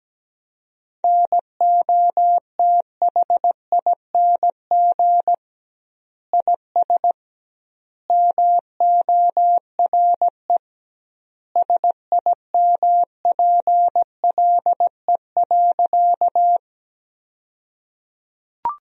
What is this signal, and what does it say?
Morse code, heard as a keyed tone, about 17 wpm, 700 Hz. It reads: NOTHING IS MORE SIMPLE.
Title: 2x Word-Spacing - Sentences from Top 500 Words - 17wpm